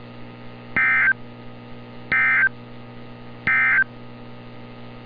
Emergency Alert System (SAME) tones, heard as harsh buzzing bursts. It is heard from 0.6 to 3.9 s.
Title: NOAA Weather Radio - Altoona, PA